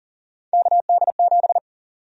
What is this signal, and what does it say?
Morse code, heard as a keyed tone, about 40 wpm, 700 Hz. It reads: KD7
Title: USA Call Sign Prefixes - 40wpm